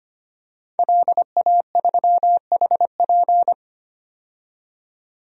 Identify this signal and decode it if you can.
Morse code, heard as a keyed tone, about 25 wpm, 700 Hz. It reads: LA3HP
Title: DXCC-100 Call Signs - 5 Characters - 25wpm